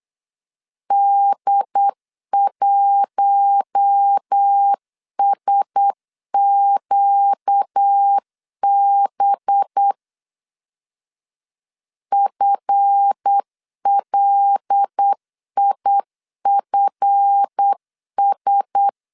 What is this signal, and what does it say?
Morse code, heard as a keyed tone, about 8 wpm, 800 Hz. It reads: D1SQB FLIFS
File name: LEZIONI / 7 GF9Z / 7 GF 8wpm 3108